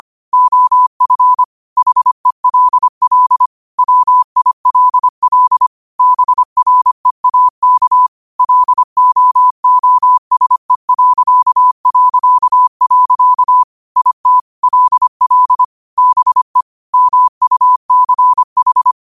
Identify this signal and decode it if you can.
Morse code, heard as a keyed tone, about 25 wpm, 1000 Hz. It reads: OF HELL WILL BREAK LOOSE... IT LL BE MUCH
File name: qotdfiles / 25wpm